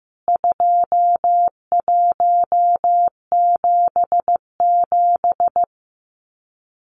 Morse code, heard as a keyed tone, about 15 wpm, 700 Hz. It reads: 2177
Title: CWT Exchange 15wpm